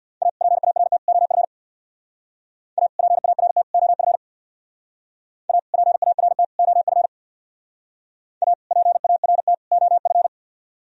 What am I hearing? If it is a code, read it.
Morse code, heard as a keyed tone, about 64 wpm, 700 Hz. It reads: A PART OF A PART OF A PART OF A PART OF